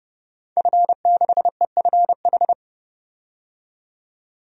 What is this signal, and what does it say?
Morse code, heard as a keyed tone, about 30 wpm, 700 Hz. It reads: F6EFH
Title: DXCC-100 Call Signs - 5 Characters - 30wpm